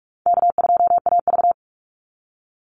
Morse code, heard as a keyed tone, about 44 wpm, 700 Hz. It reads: K2AV
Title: Call Signs - 45wpm